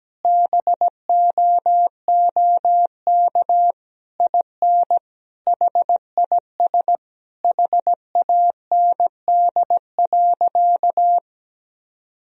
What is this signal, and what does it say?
Morse code, heard as a keyed tone, about 17 wpm, 700 Hz. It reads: BOOK IN HIS HAND.
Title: Sentences from Top 1000 Words - Review Part 1 - 17wpm